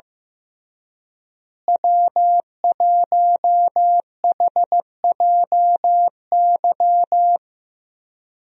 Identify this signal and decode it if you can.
Morse code, heard as a keyed tone, about 15 wpm, 700 Hz. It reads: W1HJY
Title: USA Call Signs - 5 Characters - Repeated 2x - 15wpm